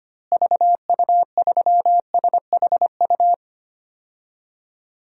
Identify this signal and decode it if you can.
Morse code, heard as a keyed tone, about 25 wpm, 700 Hz. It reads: VU3SHU
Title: DXCC-100 Call Signs - 6 Characters - Repeated 2x - 25wpm